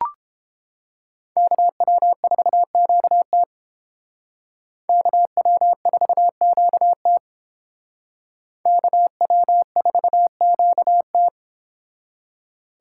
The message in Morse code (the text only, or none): KW4QT KW4QT KW4QT